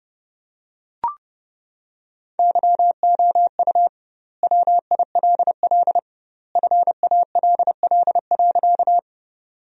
Morse code, heard as a keyed tone, about 32 wpm, 700 Hz. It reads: YOU WILL FALL.
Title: 2x Word-Spacing - Sentences from Top 500 Words - 30wpm